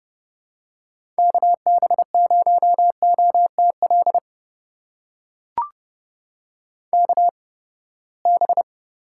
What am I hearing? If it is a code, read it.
Morse code, heard as a keyed tone, about 30 wpm, spaced as for 21 wpm, 700 Hz. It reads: KB0OTL KB
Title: USA Call Signs - 8x Character Spacing - 30wpm